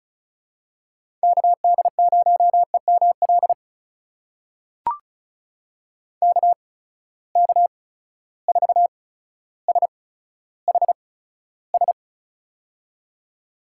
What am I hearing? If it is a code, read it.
Morse code, heard as a keyed tone, about 36 wpm, spaced as for 15 wpm, 700 Hz. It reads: KD0EML KK4SHS
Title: USA Call Signs - 6 Characters - 8x Character Spacing - 35wpm